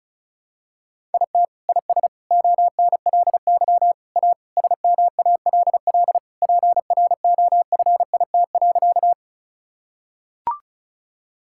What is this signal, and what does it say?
Morse code, heard as a keyed tone, about 36 wpm, 700 Hz. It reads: IT IS ONLY A SMALL PROFIT.
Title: Sentences from Top 1000 Words - Review Part 5 - 35wpm